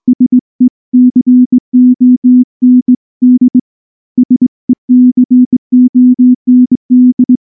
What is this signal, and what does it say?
Morse code, heard as a keyed tone, about 19 wpm, 250 Hz. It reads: SECOND SECOND
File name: morse / morse-training-20WPM-10